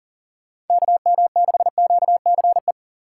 Morse code, heard as a keyed tone, about 49 wpm, 700 Hz. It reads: KM6QCE
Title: Speed Racing - USA 2x3 Call Signs 40wpm